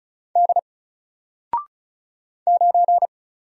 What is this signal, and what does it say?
Morse code, heard as a keyed tone, about 35 wpm, 700 Hz. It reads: D 9
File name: Single-Letter-Number-35wpm